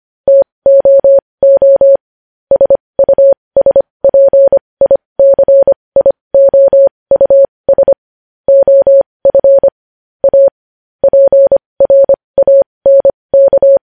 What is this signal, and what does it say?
Morse code, heard as a keyed tone, about 25 wpm, spaced as for 20 wpm, 550 Hz. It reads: TOO SUSPICIOUS OF A PRANK
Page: news-headlines-in-morse-code-at-20-wpm